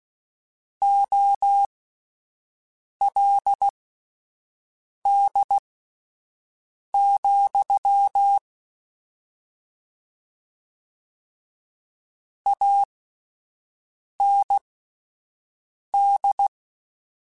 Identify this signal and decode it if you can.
Morse code, heard as a keyed tone, about 16 wpm, spaced as for 5 wpm, 800 Hz. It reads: OLD, AND
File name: QSOs Spaced